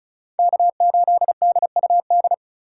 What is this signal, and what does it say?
Morse code, heard as a keyed tone, about 36 wpm, 700 Hz. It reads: K8DUD
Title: USA Call Signs - 5 Characters - 2x Character Spacing - 35wpm